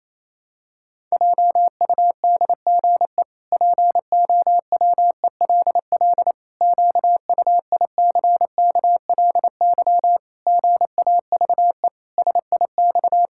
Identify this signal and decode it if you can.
Morse code, heard as a keyed tone, about 28 wpm, 700 Hz. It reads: JUDGE POWELL QUICKLY GAVE SIX
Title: Pangram Challenge v2 - 28wpm